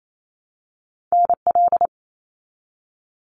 Morse code, heard as a keyed tone, about 28 wpm, 700 Hz. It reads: NL
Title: Canadian Provinces and Territories - Repeated 3x - 28wpm